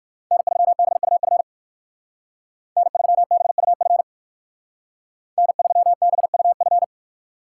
Morse code, heard as a keyed tone, about 58 wpm, 700 Hz. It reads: N3BUF N3BUF N3BUF